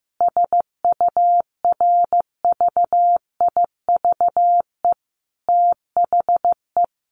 Morse code, heard as a keyed tone, about 15 wpm, 700 Hz. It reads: SURVIVE THE
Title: Sentences from Top 1300 Words 15wpm